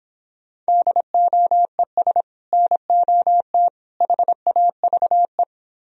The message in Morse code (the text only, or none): DOES NOT HAVE